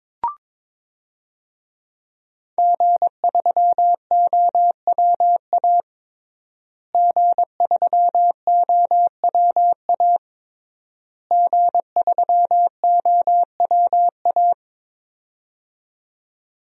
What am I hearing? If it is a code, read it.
Morse code, heard as a keyed tone, about 22 wpm, 700 Hz. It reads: G3OWA G3OWA G3OWA
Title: DXCC-100 Call Signs - 5 Characters - Repeated 3x - 22wpm